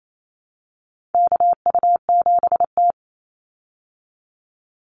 Morse code, heard as a keyed tone, about 28 wpm, 700 Hz. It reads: KU7T
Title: Call Signs Repeated 3x - 28wpm